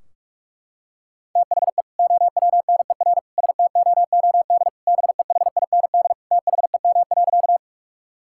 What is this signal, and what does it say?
Morse code, heard as a keyed tone, about 41 wpm, 700 Hz. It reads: THE OWNER STOOD BEHIND THEM.
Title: Sentences from Top 1600 Words 45wpm